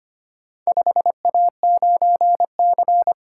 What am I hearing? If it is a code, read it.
Morse code, heard as a keyed tone, about 25 wpm, 700 Hz. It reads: 5A9C
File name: practice / 25wpm / Four Letter-Number Combinations - 25wpm